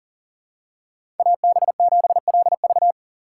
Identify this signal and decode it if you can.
Morse code, heard as a keyed tone, about 38 wpm, 700 Hz. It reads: AB7LV
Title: USA Call Signs - 5 Characters - 4x Character Spacing - 40wpm